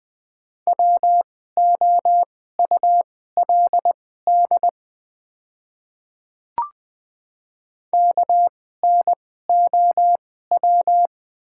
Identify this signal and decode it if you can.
Morse code, heard as a keyed tone, about 20 wpm, spaced as for 14 wpm, 690 Hz. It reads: WOULD KNOW
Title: 2x Character Spacing - Top 100 Words - 20wpm